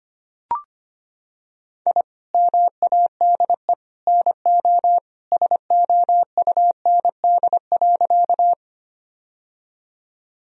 Morse code, heard as a keyed tone, about 25 wpm, 700 Hz. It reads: I MADE NO SOUND.
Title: Sentences from Top 1000 Words - Review Part 1 - 25wpm